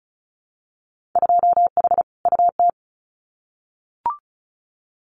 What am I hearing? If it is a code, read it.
Morse code, heard as a keyed tone, about 36 wpm, 700 Hz. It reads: 2H UT
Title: Winter Field Day Exchanges Encore 35wpm